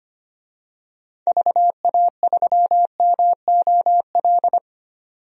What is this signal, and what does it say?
Morse code, heard as a keyed tone, about 25 wpm, 700 Hz. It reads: VA3MOL
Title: DXCC-100 Call Signs - 6 Characters - 25wpm